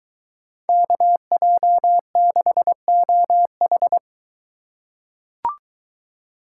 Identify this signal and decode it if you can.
Morse code, heard as a keyed tone, about 22 wpm, 700 Hz. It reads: KJ6OH